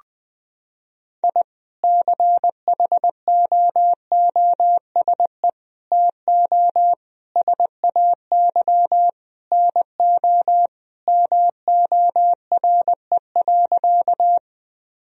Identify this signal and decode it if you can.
Morse code, heard as a keyed tone, about 20 wpm, 700 Hz. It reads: I CHOOSE TO SAY NO MORE.
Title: Sentences from Top 1000 Words - Review Part 4 - 20wpm